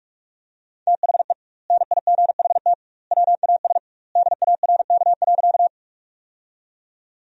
Morse code, heard as a keyed tone, about 46 wpm, 700 Hz. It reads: THE NIGHT WAS DARK.
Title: 2x Word-Spacing - Sentences from Top 500 Words - 45wpm